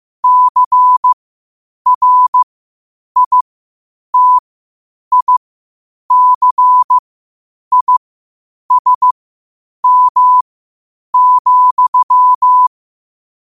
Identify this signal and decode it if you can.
Morse code, heard as a keyed tone, about 15 wpm, spaced as for 8 wpm, 1000 Hz. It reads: CRITICISM,
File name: qotdfiles / 8wpm